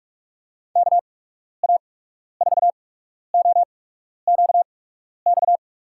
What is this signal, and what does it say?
Morse code, heard as a keyed tone, about 42 wpm, spaced as for 12 wpm, 700 Hz. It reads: KA4OQX
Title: USA Call Signs - 6 Characters - 8x Character Spacing - 45wpm